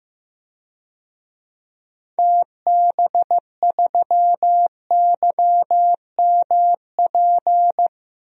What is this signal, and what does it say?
Morse code, heard as a keyed tone, about 16 wpm, 700 Hz. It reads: TB3YMP
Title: DXCC-100 Call Signs - 6 Characters - 15wpm